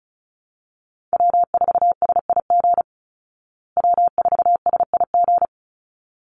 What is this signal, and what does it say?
Morse code, heard as a keyed tone, about 34 wpm, 700 Hz. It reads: W4SIG W4SIG
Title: Call Signs Repeated 3x - 35wpm